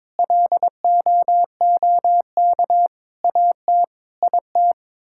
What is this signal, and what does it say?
Morse code, heard as a keyed tone, about 22 wpm, 700 Hz. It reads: LOOK AT IT
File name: practice / 22wpm / Sets of 7 Words - 22wpm